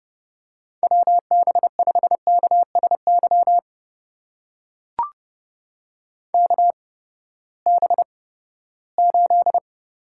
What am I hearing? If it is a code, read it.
Morse code, heard as a keyed tone, about 30 wpm, spaced as for 17 wpm, 700 Hz. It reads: WB5KSY KB8
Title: USA Call Signs - 6 Characters - 8x Character Spacing - 30wpm